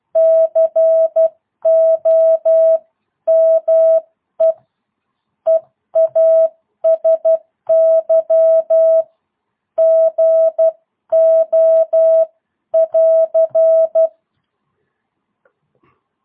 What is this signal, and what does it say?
Morse code, heard as a keyed tone, about 12 wpm, 650 Hz. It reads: COME EASY GO <AR>
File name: jh8cbh / 2022cw